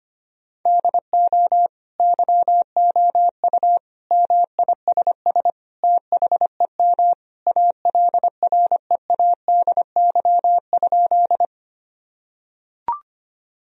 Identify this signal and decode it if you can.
Morse code, heard as a keyed tone, about 25 wpm, 700 Hz. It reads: DO YOU MISS THEM ALREADY?
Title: Sentences from Top 1000 Words - Review Part 3 - 25wpm